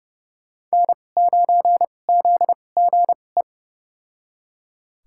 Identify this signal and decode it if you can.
Morse code, heard as a keyed tone, about 32 wpm, spaced as for 22 wpm, 690 Hz. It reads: N9ZGE
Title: USA Call Signs - 5 Characters - 2x Character Spacing - 30wpm